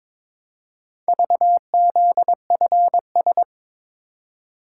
Morse code, heard as a keyed tone, about 21 wpm, 700 Hz. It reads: VZFS
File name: practice / 22wpm / Four Letter-Number Combinations - 22wpm